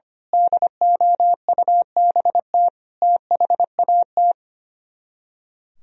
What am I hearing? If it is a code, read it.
Morse code, heard as a keyed tone, about 27 wpm, 700 Hz. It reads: DOUBT THAT
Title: Speed-Racing - Sets of 5 Words 25wpm